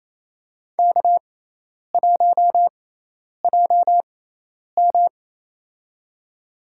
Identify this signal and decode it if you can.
Morse code, heard as a keyed tone, about 28 wpm, spaced as for 10 wpm, 700 Hz. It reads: K1JM
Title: USA Call Signs - 6x Character Spacing - 28wpm